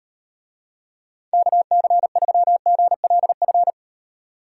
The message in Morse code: KC3GLF

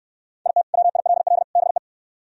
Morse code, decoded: IT WILL BE

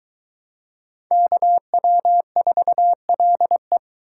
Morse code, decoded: KW4LE

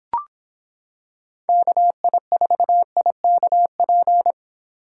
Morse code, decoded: KI4IKP

